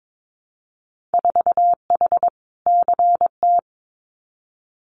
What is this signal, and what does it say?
Morse code, heard as a keyed tone, about 22 wpm, 700 Hz. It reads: 4H CT
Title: Winter Field Day Exchanges Encore 22wpm